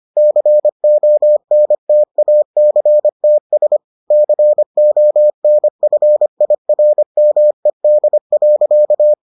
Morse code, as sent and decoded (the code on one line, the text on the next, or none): -.-. --- -. - .- -.-. - ... / -.-. --- -. ..-. .. .-. -- . -.. .-.-.-
CONTACTS CONFIRMED.